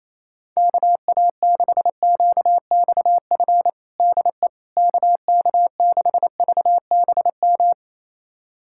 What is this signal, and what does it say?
Morse code, heard as a keyed tone, about 28 wpm, 700 Hz. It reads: KA6QXF DE KK6VBM
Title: USA Callsign DE Callsign - 6 Characters - 28wpm